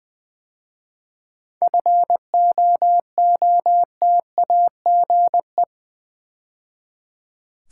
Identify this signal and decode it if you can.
Morse code, heard as a keyed tone, about 20 wpm, 700 Hz. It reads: FOOTAGE
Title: 7 Letter Words - First 3 Letters Given - 20wpm